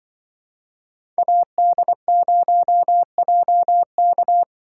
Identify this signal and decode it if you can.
Morse code, heard as a keyed tone, about 24 wpm, 700 Hz. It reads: AD0JK
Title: Speed Racing - USA 2x2 Call Signs 20wpm